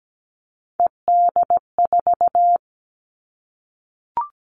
Morse code, heard as a keyed tone, about 17 wpm, 700 Hz. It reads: ED4
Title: All Cut Numbers - 3-Digit 17wpm